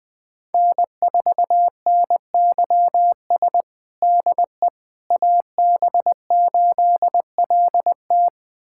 Morse code, heard as a keyed tone, about 20 wpm, 700 Hz. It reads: N4NYS DE AB8LT
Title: USA Callsign DE Callsign - 5 Characters - 20wpm